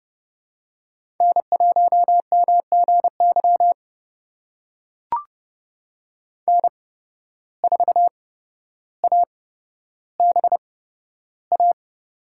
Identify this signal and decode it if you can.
Morse code, heard as a keyed tone, about 32 wpm, spaced as for 13 wpm, 700 Hz. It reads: N1MGY N4ABA
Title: USA Call Signs - 5 Characters - 8x Character Spacing - 30wpm